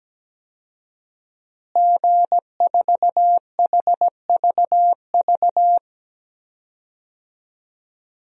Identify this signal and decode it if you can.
Morse code, heard as a keyed tone, about 17 wpm, 700 Hz. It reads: G4HVV
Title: DXCC-100 Call Signs - 5 Characters - Repeated 3x - 17wpm